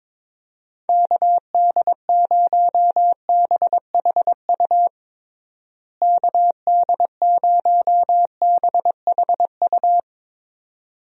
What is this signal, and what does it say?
Morse code, heard as a keyed tone, about 22 wpm, 700 Hz. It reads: KD0BHU KD0BHU
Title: USA Call Signs - 6 Characters - Repeated 2x - 22wpm